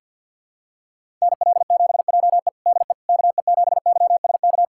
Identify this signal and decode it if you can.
Morse code, heard as a keyed tone, about 52 wpm, 700 Hz. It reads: NL7JE DE KE7YSK
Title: USA Callsign DE Callsign 50wpm